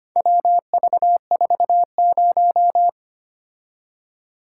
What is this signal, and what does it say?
Morse code, heard as a keyed tone, about 25 wpm, 700 Hz. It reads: WV40
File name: practice / 25wpm / Four Letter-Number Combinations - 25wpm